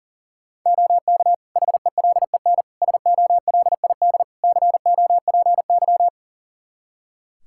Morse code, heard as a keyed tone, about 37 wpm, 700 Hz. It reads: OK HELEN SOLID COPY
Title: Practice QSO - Exchange 3 - More Info - 40wpm